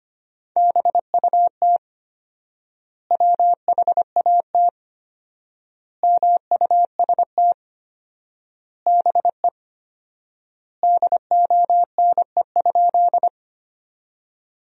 Morse code, heard as a keyed tone, about 25 wpm, 700 Hz. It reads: BUT WHAT MUST BE DONE?
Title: 4x Word-Spacing - Sentences from Top 500 Words - 25wpm